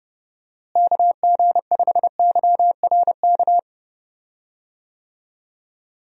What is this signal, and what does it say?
Morse code, heard as a keyed tone, about 30 wpm, 700 Hz. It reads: KG5YRK
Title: USA Call Signs - 6 Characters - Repeated 3x - 30wpm